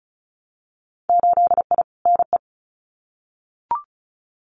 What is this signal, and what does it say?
Morse code, heard as a keyed tone, about 34 wpm, 700 Hz. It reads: 8I NE
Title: Winter Field Day Exchanges Encore 35wpm